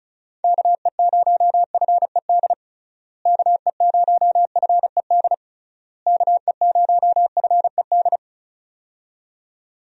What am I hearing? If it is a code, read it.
Morse code, heard as a keyed tone, about 36 wpm, 700 Hz. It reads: KE0FED KE0FED KE0FED
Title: USA Call Signs - 6 Characters - Repeated 3x - 35wpm